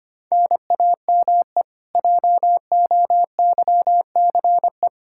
Morse code, heard as a keyed tone, about 25 wpm, 700 Hz. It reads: NAME JOYCE
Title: Practice QSO - Exchange 1 - Beginning - 25wpm